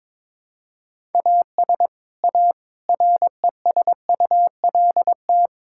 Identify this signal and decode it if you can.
Morse code, heard as a keyed tone, about 22 wpm, 700 Hz. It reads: AS A RESULT
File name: practice / 22wpm / Build A Phrase - 5-Word Phrases - 22wpm